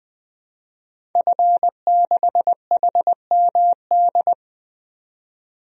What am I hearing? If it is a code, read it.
Morse code, heard as a keyed tone, about 20 wpm, 700 Hz. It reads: F6HMD
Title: DXCC-100 Call Signs - 5 Characters - 20wpm